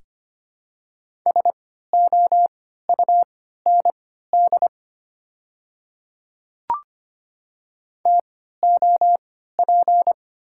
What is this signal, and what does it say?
Morse code, heard as a keyed tone, about 25 wpm, spaced as for 14 wpm, 690 Hz. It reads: SOUND TOP
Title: 3x Character Spacing - Top 500 Words - 25wpm